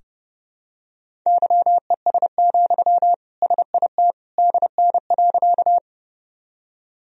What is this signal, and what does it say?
Morse code, heard as a keyed tone, about 28 wpm, 700 Hz. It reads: YES, SIT DN.
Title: Sentences from Top 500 Words and Abbreviations 30wpm